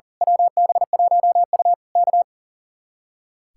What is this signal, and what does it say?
Morse code, heard as a keyed tone, about 38 wpm, 700 Hz. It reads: WB1U K
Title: Speed-Racing Common QSO Phrases Encore - 40wpm